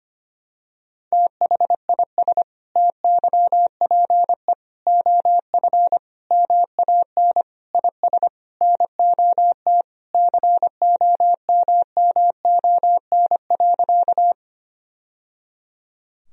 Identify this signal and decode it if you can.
Morse code, heard as a keyed tone, about 25 wpm, 700 Hz. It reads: THIS TYPE OF MAN IS NOT COMMON.
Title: Sentences from Top 1000 Words - Review Part 3 - 25wpm